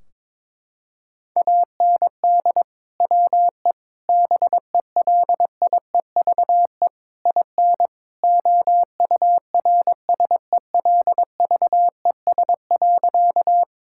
Morse code, heard as a keyed tone, about 22 wpm, 700 Hz. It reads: AND WE BELIEVE IN OURSELVES.
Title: Sentences from Top 1600 Words 22wpm